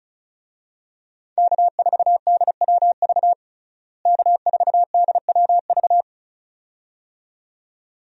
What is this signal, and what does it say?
Morse code, heard as a keyed tone, about 34 wpm, 700 Hz. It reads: K4DWV K4DWV
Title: USA Call Signs - 5 Characters - Repeated 2x - 35wpm